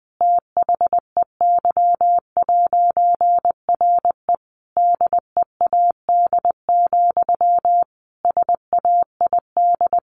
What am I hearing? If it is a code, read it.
Morse code, heard as a keyed tone, about 20 wpm, 700 Hz. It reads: THEY'RE DEAD, SAID
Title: A Sound Of Thunder 20wpm